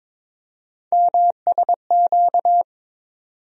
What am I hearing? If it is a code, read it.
Morse code, heard as a keyed tone, about 22 wpm, 700 Hz. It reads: MSQ
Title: USA Call Sign Postfixes - 3 Letters 22wpm